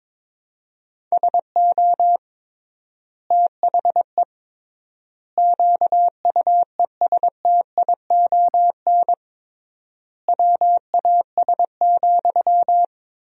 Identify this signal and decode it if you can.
Morse code, heard as a keyed tone, about 22 wpm, 700 Hz. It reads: SO THE QUESTION WAS,
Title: 3x Word-Spacing - Sentences from Top 400 Words - 22wpm